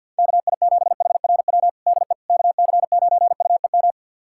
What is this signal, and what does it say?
Morse code, heard as a keyed tone, about 50 wpm, 700 Hz. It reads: KI7HRW DE KC9UEM
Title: USA Callsign DE Callsign - 6 Characters - 50wpm